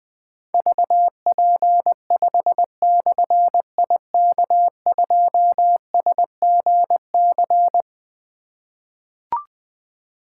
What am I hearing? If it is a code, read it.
Morse code, heard as a keyed tone, about 20 wpm, 700 Hz. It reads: VP5/IK2SGC
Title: Difficult Call Signs 20wpm